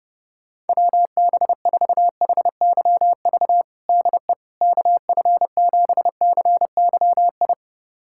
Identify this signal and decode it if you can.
Morse code, heard as a keyed tone, about 31 wpm, 700 Hz. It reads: WB4HYV DE KF7CYI
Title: USA Callsign DE Callsign - 6 Characters - 30wpm